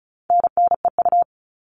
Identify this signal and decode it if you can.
Morse code, heard as a keyed tone, about 34 wpm, 700 Hz. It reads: NNEU
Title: All Cut Numbers - 4-Digit 35wpm